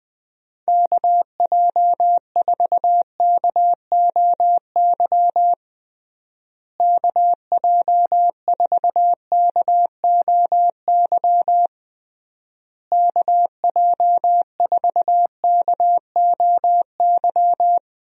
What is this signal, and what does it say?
Morse code, heard as a keyed tone, about 21 wpm, 700 Hz. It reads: KJ4KOY KJ4KOY KJ4KOY
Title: USA Call Signs - 6 Characters - Repeated 3x - 20wpm